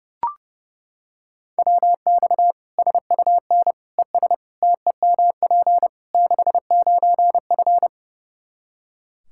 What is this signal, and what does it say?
Morse code, heard as a keyed tone, about 30 wpm, 700 Hz. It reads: WX SUN ES TEMP 69F
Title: K1USN SST QSO Practice 30wpm